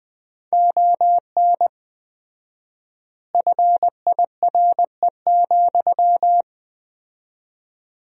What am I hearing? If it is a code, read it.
Morse code, heard as a keyed tone, about 20 wpm, 700 Hz. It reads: ON FIRE,
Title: 4x Word-Spacing - Sentences from Top 400 Words - 20wpm